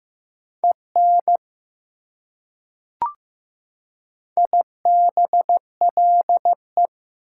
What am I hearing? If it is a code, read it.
Morse code, heard as a keyed tone, about 15 wpm, 700 Hz. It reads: EN IBLE
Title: Prefixes and Suffixes 15wpm